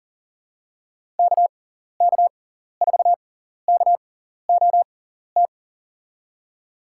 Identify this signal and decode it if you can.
Morse code, heard as a keyed tone, about 40 wpm, spaced as for 14 wpm, 700 Hz. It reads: KK4KOT
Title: USA Call Signs - 6 Characters - 6x Character Spacing - 40wpm